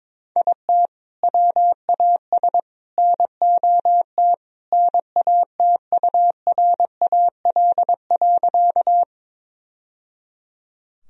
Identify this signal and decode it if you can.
Morse code, heard as a keyed tone, about 22 wpm, 700 Hz. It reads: IT WAS NOT NATURAL.
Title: Sentences from Top 1000 Words - Review Part 4 - 22wpm